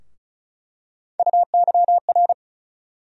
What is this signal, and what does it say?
Morse code, heard as a keyed tone, about 36 wpm, 700 Hz. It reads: UYR
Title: USA Call Sign Postfixes - 3 Letters 35wpm